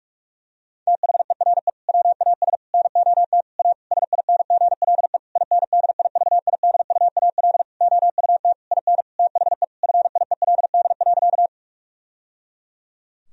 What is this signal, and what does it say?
Morse code, heard as a keyed tone, about 46 wpm, 700 Hz. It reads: THERE WAS NOT A SINGLE INDIVIDUAL OUT IN THE FIELD.
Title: Sentences from Top 1000 Words - Review Part 4 - 45wpm